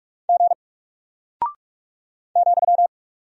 Morse code, heard as a keyed tone, about 44 wpm, 700 Hz. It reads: G ,
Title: Single Character (Letter:Number:Punctuation:Prosign) Alt 45wpm